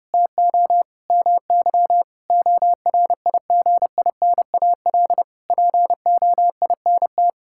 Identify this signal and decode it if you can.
Morse code, heard as a keyed tone, about 30 wpm, 700 Hz. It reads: TO MY ORIGINAL POINT
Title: Sentences from Top 1000 Words - Review Part 6 - 30wpm